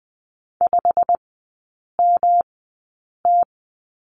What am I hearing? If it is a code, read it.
Morse code, heard as a keyed tone, about 20 wpm, spaced as for 8 wpm, 700 Hz. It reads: 5MT
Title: 107 Lesson 18e - Letter m - Three Characters 20wpm